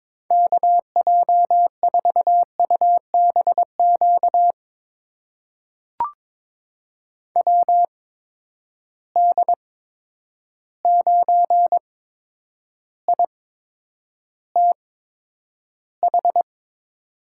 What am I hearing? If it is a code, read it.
Morse code, heard as a keyed tone, about 21 wpm, spaced as for 9 wpm, 700 Hz. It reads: KJ4UBQ WD9ITH
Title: USA Call Signs - 6 Characters - 8x Character Spacing - 22wpm